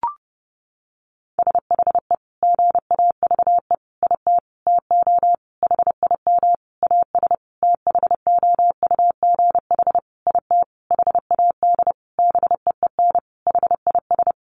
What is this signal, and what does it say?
Morse code, heard as a keyed tone, about 30 wpm, 700 Hz. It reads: SHE GAVE IT TO HIM AS THOUGH IT HAD BEEN HIS
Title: Sentences from Top 1500 Words 30wpm